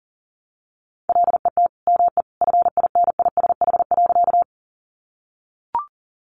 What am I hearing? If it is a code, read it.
Morse code, heard as a keyed tone, about 40 wpm, 700 Hz. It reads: LET ME FINISH.
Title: Sentences from Top 1200 Words 40wpm